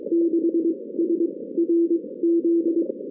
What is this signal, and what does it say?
Morse code, heard as a keyed tone, about 26 wpm, 340 Hz. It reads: 6 S R Z
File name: 2016 arrl dx cw / 2016-ARRL-DX-CW-K5TR- - 0-41